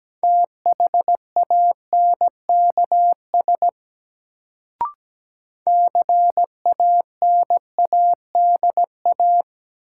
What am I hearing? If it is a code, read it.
Morse code, heard as a keyed tone, about 17 wpm, 700 Hz. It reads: THANKS CANADA
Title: Sets of 5 Words - New Word Drill - 17wpm